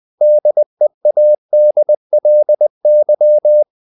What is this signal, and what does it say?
Morse code, heard as a keyed tone, about 20 wpm, 600 Hz. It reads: DEADLY
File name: the old man and the sea / the old man and the sea part 1 20wpm